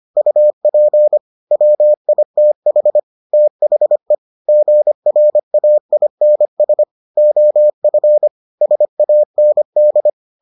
Morse code, heard as a keyed tone, about 25 wpm, 600 Hz. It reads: UP WITH THE GRAINS OF SAND